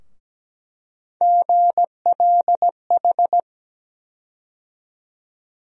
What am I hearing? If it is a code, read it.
Morse code, heard as a keyed tone, about 17 wpm, 700 Hz. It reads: GLH
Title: USA Call Sign Postfixes - 3 Letters 17wpm